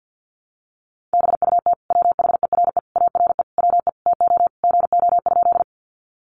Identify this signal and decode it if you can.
Morse code, heard as a keyed tone, about 50 wpm, 700 Hz. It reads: BUT WHERE ARE WE TO GO?
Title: Sentences from Top 200 Words 50wpm